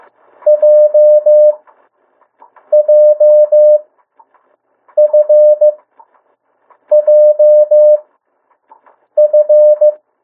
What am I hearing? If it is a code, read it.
Morse code, heard as a keyed tone, about 15 wpm, spaced as for 6 wpm, 600 Hz. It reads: JJFJF
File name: morse mp3 / 02 urok 1